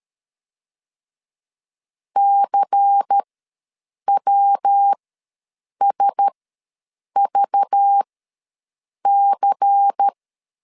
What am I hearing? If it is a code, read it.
Morse code, heard as a keyed tone, about 13 wpm, spaced as for 7 wpm, 800 Hz. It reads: CWSVC
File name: LEZIONI / 5 LBVC / 5 LBVC ws 12wpm 3159